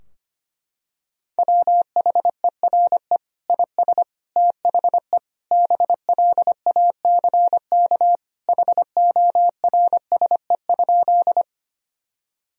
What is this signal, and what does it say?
Morse code, heard as a keyed tone, about 25 wpm, 700 Hz. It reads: WHERE IS THE BLACK HORSE?
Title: Sentences from Top 1000 Words - Review Part 2 - 25wpm